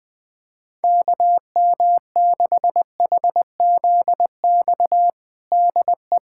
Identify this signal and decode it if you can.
Morse code, heard as a keyed tone, about 20 wpm, 700 Hz. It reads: KM6HZX DE
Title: USA Callsign DE Callsign 20wpm